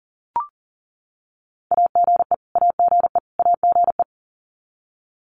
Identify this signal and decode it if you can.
Morse code, heard as a keyed tone, about 38 wpm, 700 Hz. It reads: AGE AGE AGE
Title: Top 500 Words Repeated 40wpm